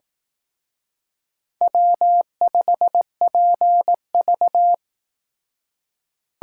Morse code, heard as a keyed tone, about 18 wpm, 700 Hz. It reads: W5PV
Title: Speed Racing - USA 1x2 Call Signs 15wpm